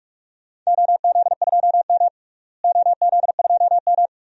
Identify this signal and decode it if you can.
Morse code, heard as a keyed tone, about 47 wpm, 700 Hz. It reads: OZ2M OZ2M
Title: DXCC-100 Call Signs - 4 Characters - Repeated 2x - 45wpm